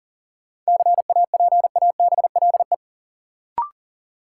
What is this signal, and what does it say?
Morse code, heard as a keyed tone, about 40 wpm, 700 Hz. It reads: CAPABLE
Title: Sentences from Top 1200 Words - New Word Drill Alt - 40wpm